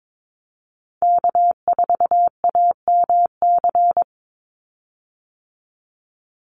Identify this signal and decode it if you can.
Morse code, heard as a keyed tone, about 22 wpm, 700 Hz. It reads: K4AMC
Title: Call Signs Repeated 2x - 22wpm